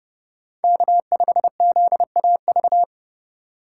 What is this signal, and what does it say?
Morse code, heard as a keyed tone, about 30 wpm, 690 Hz. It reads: K5ZAV
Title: USA Call Signs - 5 Characters - 2x Character Spacing - 30wpm